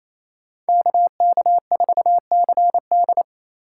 Morse code, heard as a keyed tone, about 37 wpm, 700 Hz. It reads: KK4CD